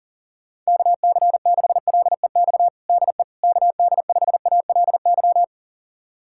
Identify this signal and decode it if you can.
Morse code, heard as a keyed tone, about 40 wpm, 700 Hz. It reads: KC6LEX DE KD5ALY
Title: USA Callsign DE Callsign 40wpm